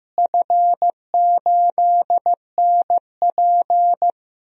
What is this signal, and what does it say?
Morse code, heard as a keyed tone, about 16 wpm, 700 Hz. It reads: F8NP
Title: DXCC-100 Call Signs - 4 Characters - Repeated 3x - 15wpm